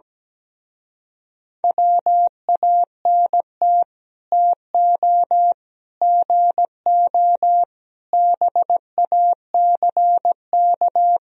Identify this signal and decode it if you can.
Morse code, heard as a keyed tone, about 17 wpm, 700 Hz. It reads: WANT TO GO BACK
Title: Speed-Racing - Sets of 5 Words 17wpm